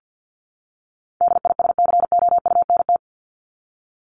Morse code, heard as a keyed tone, about 50 wpm, 700 Hz. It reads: DISCOUNT
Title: Top 1001-1250 Words 50wpm